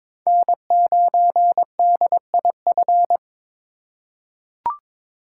Morse code, heard as a keyed tone, about 22 wpm, 700 Hz. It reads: N9DIF
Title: USA Call Signs - 5 Characters - 2x Character Spacing - 22wpm